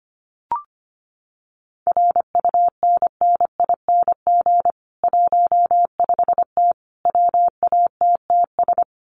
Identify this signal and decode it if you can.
Morse code, heard as a keyed tone, about 25 wpm, 700 Hz. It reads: RUNNING 15T WATTS
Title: Common QSO Phrases 25wpm